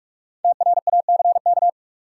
Morse code, heard as a keyed tone, about 46 wpm, 700 Hz. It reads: TRACK
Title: Five Letter Words - 45wpm